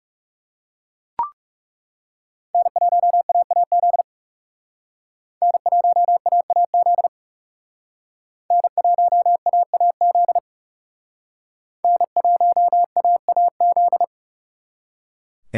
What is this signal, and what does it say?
Morse code, heard as a keyed tone, about 37 wpm, 700 Hz. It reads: N1AAZ N1AAZ N1AAZ N1AAZ
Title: Speed Racing - USA 1x3 Call Signs 30wpm